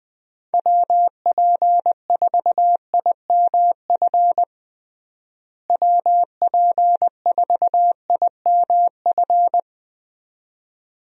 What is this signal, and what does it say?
Morse code, heard as a keyed tone, about 20 wpm, 700 Hz. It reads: WP4IMF WP4IMF
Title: USA Call Signs - 6 Characters - Repeated 2x - 20wpm